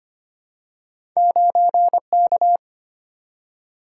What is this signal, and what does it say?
Morse code, heard as a keyed tone, about 25 wpm, 690 Hz. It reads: 9K
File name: Two Letter-Number Combinations - 25wpm